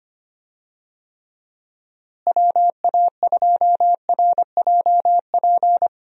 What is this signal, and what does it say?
Morse code, heard as a keyed tone, about 25 wpm, 700 Hz. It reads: WA2RJP
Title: USA Call Signs - 6 Characters - Repeated 2x - 25wpm